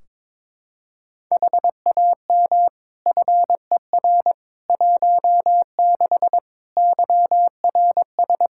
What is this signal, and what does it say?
Morse code, heard as a keyed tone, about 22 wpm, 700 Hz. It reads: HAM FER 16 YRS
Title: Practice QSO - Exchange 3 - More Info - 22wpm